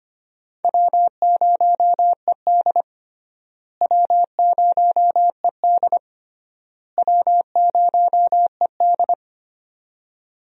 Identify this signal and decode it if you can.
Morse code, heard as a keyed tone, about 25 wpm, 700 Hz. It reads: W0ED W0ED W0ED
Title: USA Call Signs - 4 Characters - Repeated 3x - 25wpm